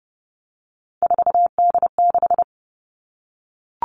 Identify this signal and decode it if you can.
Morse code, heard as a keyed tone, about 30 wpm, 700 Hz. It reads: 4D6
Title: All Cut Numbers - 3-Digit 30wpm